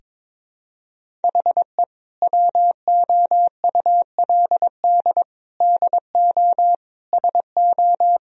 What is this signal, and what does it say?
Morse code, heard as a keyed tone, about 22 wpm, 700 Hz. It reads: HE WOULD DO SO
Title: Sentences from Top 1000 Words - Review Part 1 - 22wpm